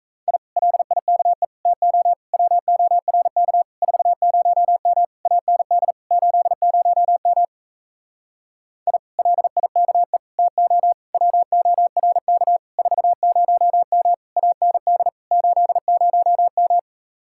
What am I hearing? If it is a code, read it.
Morse code, heard as a keyed tone, about 40 wpm, 700 Hz. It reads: I LIKE TO WORK 40M AND 80M I LIKE TO WORK 40M AND 80M